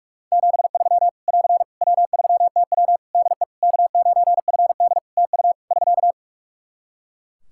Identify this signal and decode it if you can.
Morse code, heard as a keyed tone, about 46 wpm, 700 Hz. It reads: 73 <AR> W3TW DE K9FD TU <SK>
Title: Practice QSO - Exchange 4 - Ending - 45wpm